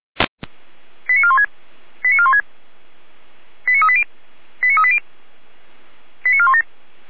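Five-tone selective call (ZVEI), heard as quick runs of five tones, tones of about 70 ms. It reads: 89416 89416 89380 89380 89417
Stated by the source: ZVEI five-tone selective call 89175 received with an SDR.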